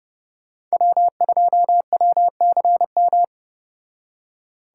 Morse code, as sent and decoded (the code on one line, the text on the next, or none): .-- ..--- .-- -.-. --
W2WCM